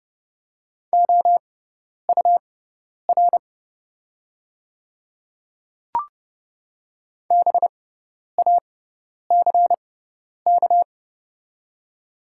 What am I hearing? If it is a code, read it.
Morse code, heard as a keyed tone, about 30 wpm, spaced as for 10 wpm, 700 Hz. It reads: OUR BACK
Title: 6x Character Spacing - Top 200 Words - 30wpm